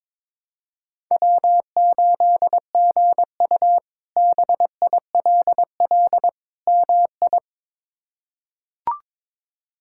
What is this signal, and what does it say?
Morse code, heard as a keyed tone, about 21 wpm, 700 Hz. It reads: W8GU BILL MI
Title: K1USN SST QSO Practice - 22wpm